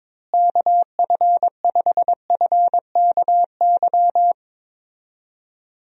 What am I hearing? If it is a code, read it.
Morse code, heard as a keyed tone, about 21 wpm, 700 Hz. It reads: KF5FKY
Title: USA Call Signs - 6 Characters - Repeated 2x - 22wpm